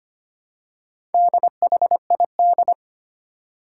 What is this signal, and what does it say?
Morse code, heard as a keyed tone, about 25 wpm, 700 Hz. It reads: DHID
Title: Four Challenging Letter-Number Combinations 25wpm